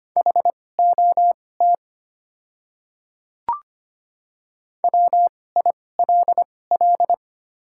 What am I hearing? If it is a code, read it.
Morse code, heard as a keyed tone, about 25 wpm, spaced as for 18 wpm, 700 Hz. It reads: HOT WILL
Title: 2x Character Spacing - Top 100 Words - 25wpm